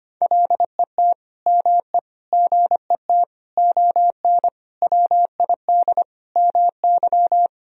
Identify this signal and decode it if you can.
Morse code, heard as a keyed tone, about 25 wpm, 700 Hz. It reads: LET ME GET ON WID MY
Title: Sentences from Top 300 Words and Abbreviations 25wpm